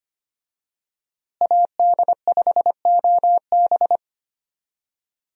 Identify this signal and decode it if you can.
Morse code, heard as a keyed tone, about 25 wpm, 700 Hz. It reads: AD5OB
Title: USA Call Signs - 5 Characters - Repeated 2x - 25wpm